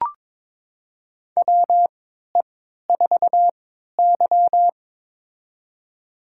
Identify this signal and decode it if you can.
Morse code, heard as a keyed tone, about 22 wpm, spaced as for 13 wpm, 700 Hz. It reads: WE4Y
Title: USA Call Signs - 4 Characters - 3x Character Spacing - 22wpm